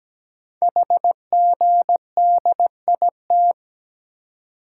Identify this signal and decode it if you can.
Morse code, heard as a keyed tone, about 17 wpm, 700 Hz. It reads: HGDIT
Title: Code Groups - 5 Letters - 17wpm